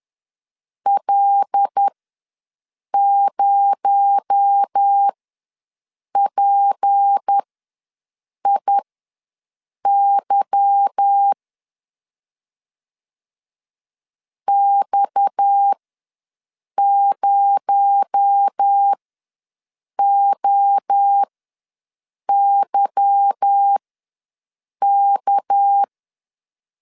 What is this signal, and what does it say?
Morse code, heard as a keyed tone, about 10 wpm, spaced as for 6 wpm, 800 Hz. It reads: L0PIY X0OYK